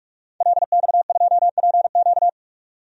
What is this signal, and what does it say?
Morse code, heard as a keyed tone, about 46 wpm, 700 Hz. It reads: LX2PQ
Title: DXCC-100 Call Signs - 5 Characters - 45wpm